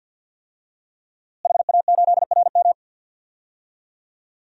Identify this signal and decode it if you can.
Morse code, heard as a keyed tone, about 52 wpm, 700 Hz. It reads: HA8RM
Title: DXCC-100 Call Signs - 5 Characters - Repeated 2x - 50wpm